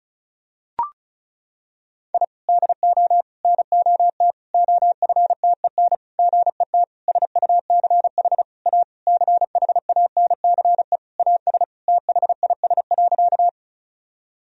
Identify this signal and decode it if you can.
Morse code, heard as a keyed tone, about 36 wpm, 700 Hz. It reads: I DO NOT OFTEN GET SUCH A CHANCE AS THIS.
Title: Sentences from Top 1000 Words - Review Part 1 - 35wpm